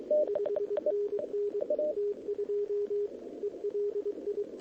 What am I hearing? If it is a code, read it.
Morse code, heard as a keyed tone, about 22 wpm, 440 Hz, with radio noise. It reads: 5NN T2 ELI
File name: OK QRP 2026 / OKQRP2026